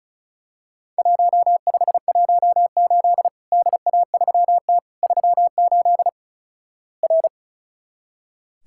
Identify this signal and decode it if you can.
Morse code, heard as a keyed tone, about 34 wpm, 700 Hz. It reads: 1518 DA3T 38 R
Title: Worked All Europe DX Contest - QTC Exchange - 35wpm